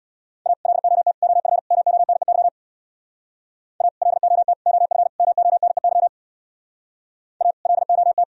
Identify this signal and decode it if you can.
Morse code, heard as a keyed tone, about 64 wpm, 700 Hz. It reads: A LOT OF MONEY A LOT OF MONEY A LOT